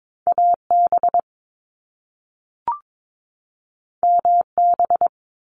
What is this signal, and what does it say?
Morse code, heard as a keyed tone, about 22 wpm, 700 Hz. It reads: AB MB
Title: Canadian Provinces and Territories 22wpm